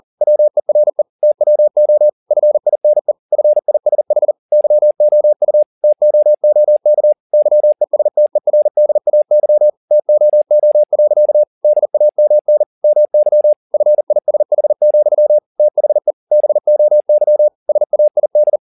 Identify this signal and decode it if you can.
Morse code, heard as a keyed tone, about 40 wpm, 600 Hz. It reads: WERE TWO FINE FISH YOU TOOK YESTERDAY TOO. DAMN MY FISH, THE BOY SAID
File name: the old man and the sea 2 40wpm